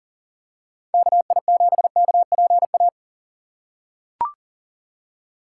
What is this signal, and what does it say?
Morse code, heard as a keyed tone, about 37 wpm, 700 Hz. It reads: KI7KPA
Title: USA Call Signs - 8x Character Spacing - 40wpm